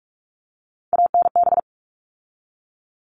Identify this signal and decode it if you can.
Morse code, heard as a keyed tone, about 44 wpm, 700 Hz. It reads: ANB
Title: All Cut Numbers - 3-Digit 45wpm